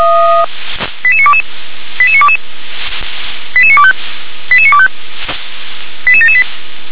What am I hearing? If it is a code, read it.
Five-tone selective call (ZVEI), heard as quick runs of five tones, tones of about 80 ms. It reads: 88922 88922 88925 88925 88778